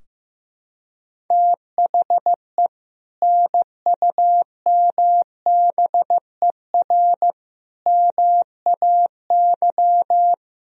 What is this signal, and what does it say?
Morse code, heard as a keyed tone, about 15 wpm, 700 Hz. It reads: THE NUMBER MAY
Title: Sentences from Top 1000 Words - Review Part 4 - 15wpm